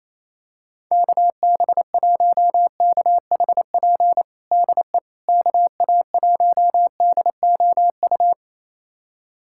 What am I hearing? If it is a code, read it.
Morse code, heard as a keyed tone, about 28 wpm, 700 Hz. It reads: KB1KHP DE KA1DOU
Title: USA Callsign DE Callsign 28wpm